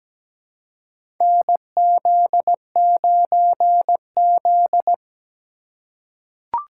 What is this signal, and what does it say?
Morse code, heard as a keyed tone, about 17 wpm, 700 Hz. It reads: NZ9Z
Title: USA Call Signs - 4 Characters - 4x Character Spacing - 17wpm